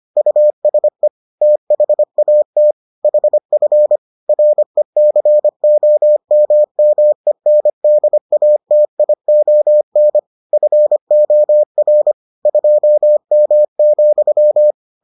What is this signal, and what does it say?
Morse code, heard as a keyed tone, about 25 wpm, 600 Hz. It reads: USE THAT HF RECOMMENDATION FOR 2M,